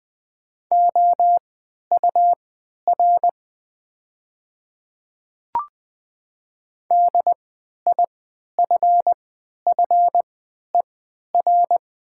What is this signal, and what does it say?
Morse code, heard as a keyed tone, about 20 wpm, spaced as for 11 wpm, 700 Hz. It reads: OUR DIFFER
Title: 3x Character Spacing - Top 200 Words - 20wpm